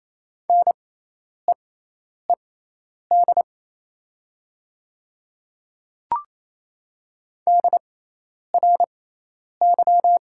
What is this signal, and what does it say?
Morse code, heard as a keyed tone, about 28 wpm, spaced as for 10 wpm, 700 Hz. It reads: NEED DRY
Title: 6x Character Spacing - Top 500 Words - 28wpm